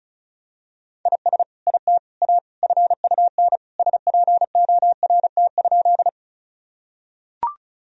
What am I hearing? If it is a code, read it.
Morse code, heard as a keyed tone, about 36 wpm, 700 Hz. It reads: IS IT A FUN SPORT?
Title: Sentences from Top 1000 Words - Review Part 4 - 35wpm